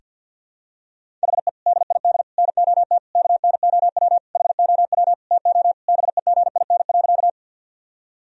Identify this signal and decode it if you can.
Morse code, heard as a keyed tone, about 50 wpm, 700 Hz. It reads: HE DID NOT KNOW HOW TO BEGIN.
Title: Sentences from Top 1000 Words - Review Part 1 - 50wpm